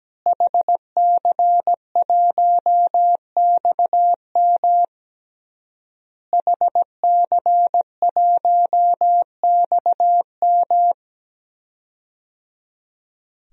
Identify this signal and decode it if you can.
Morse code, heard as a keyed tone, about 17 wpm, 700 Hz. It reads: HC1XM HC1XM
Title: DXCC-100 Call Signs - 5 Characters - Repeated 3x - 17wpm